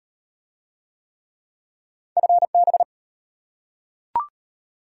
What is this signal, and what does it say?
Morse code, heard as a keyed tone, about 34 wpm, 700 Hz. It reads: FB